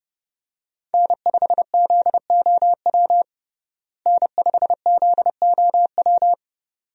Morse code, heard as a keyed tone, about 30 wpm, 700 Hz. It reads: N5ZOW N5ZOW
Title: USA Call Signs - 5 Characters - Repeated 2x - 30wpm